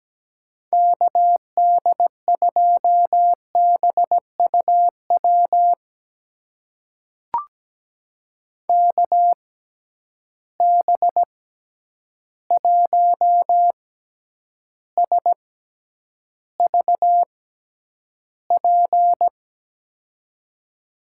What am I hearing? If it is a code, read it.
Morse code, heard as a keyed tone, about 17 wpm, spaced as for 9 wpm, 700 Hz. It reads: KD2BUW KB1SVP
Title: USA Call Signs - 6 Characters - 6x Character Spacing - 17wpm